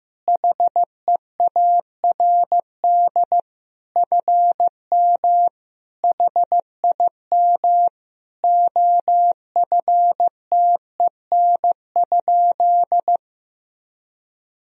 Morse code, heard as a keyed tone, about 15 wpm, 700 Hz. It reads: HEARD FM HIM OFTEN?
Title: Sentences from Top 400 Words and Abbreviations 15wpm